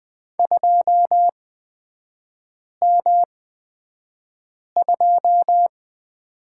Morse code, heard as a keyed tone, about 20 wpm, spaced as for 5 wpm, 700 Hz. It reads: 2M2
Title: 243 Lesson 39e - Letter z - Three Characters 20wpm